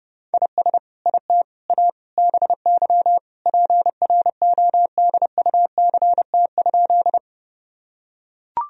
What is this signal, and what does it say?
Morse code, heard as a keyed tone, about 30 wpm, 700 Hz. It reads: IS IT A BY PRODUCT?
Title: Sentences from Top 1000 Words - Review Part 2 - 30wpm